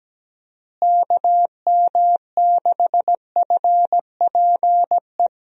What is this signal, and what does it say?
Morse code, heard as a keyed tone, about 17 wpm, 700 Hz. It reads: KM6FPE
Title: USA Call Signs - 6 Characters - Repeated 2x - 17wpm